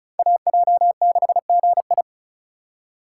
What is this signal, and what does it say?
Morse code, heard as a keyed tone, about 36 wpm, 700 Hz. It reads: AJ6GI
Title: USA Call Signs - 5 Characters - 2x Character Spacing - 35wpm